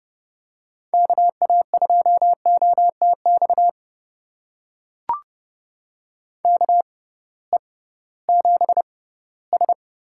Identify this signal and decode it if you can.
Morse code, heard as a keyed tone, about 30 wpm, spaced as for 17 wpm, 700 Hz. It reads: KA2OTX KE7S
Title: USA Call Signs - 6 Characters - 6x Character Spacing - 30wpm